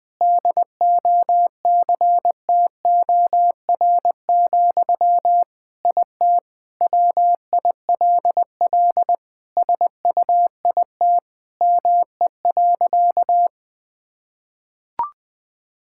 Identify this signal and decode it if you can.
Morse code, heard as a keyed tone, about 20 wpm, 700 Hz. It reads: DOCTOR, IT WILL SUIT ME.
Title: Sentences from Top 1600 Words 20wpm